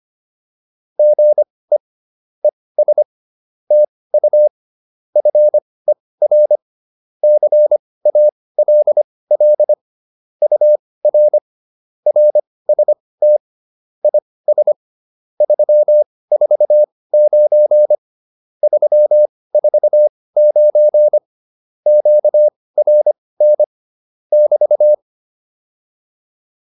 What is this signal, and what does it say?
Morse code, heard as a keyed tone, about 25 wpm, spaced as for 18 wpm, 600 Hz. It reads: GE ES TU FER CALL UR RST IS 349 349 QRN <BT>